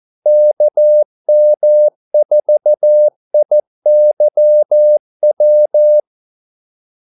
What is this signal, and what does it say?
Morse code, heard as a keyed tone, about 14 wpm, 600 Hz. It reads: KM4IYW
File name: Callsigns-14wpm